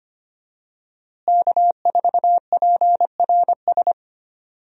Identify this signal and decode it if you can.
Morse code, heard as a keyed tone, about 25 wpm, 700 Hz. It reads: K4PRS
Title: USA Call Signs - 5 Characters - Repeated 3x - 25wpm